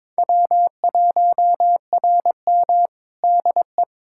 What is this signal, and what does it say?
Morse code, heard as a keyed tone, about 22 wpm, 700 Hz. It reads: W1RM DE
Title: K1USN SST QSO Practice 22wpm